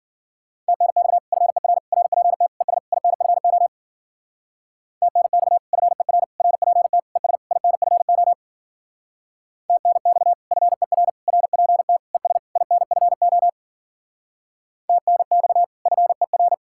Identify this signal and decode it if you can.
Morse code, heard as a keyed tone, about 51 wpm, 700 Hz. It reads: TNX FER RPT ES INFO TNX FER RPT ES INFO TNX FER RPT ES INFO TNX FER